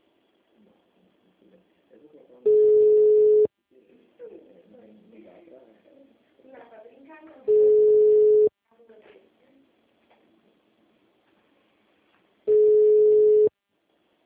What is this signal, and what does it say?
Ringback tone (Europe, 425 Hz), with speech.
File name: out- -2002- - - .15901